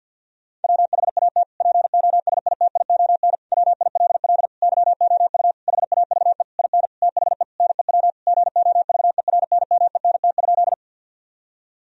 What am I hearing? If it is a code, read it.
Morse code, heard as a keyed tone, about 50 wpm, 700 Hz. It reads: WHAT POSITION WILL YOU HAVE IN THE NEW GOVERNMENT?
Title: Sentences from Top 1000 Words - Review Part 3 - 50wpm